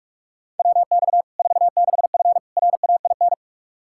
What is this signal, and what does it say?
Morse code, heard as a keyed tone, about 46 wpm, 700 Hz. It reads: WX 46F RAIN
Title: Practice QSO - Exchange 2 - Info - 45wpm